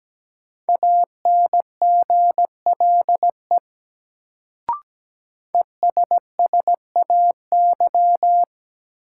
Morse code, heard as a keyed tone, about 17 wpm, 700 Hz. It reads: ANGLE ESSAY
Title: Sentences from Top 1500 Words - New Word Drill - 17wpm